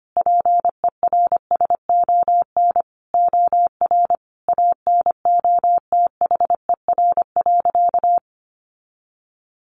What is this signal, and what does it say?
Morse code, heard as a keyed tone, about 25 wpm, 700 Hz. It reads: PERSON OR ANOTHER.
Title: Sentences from Top 1400 Words 25wpm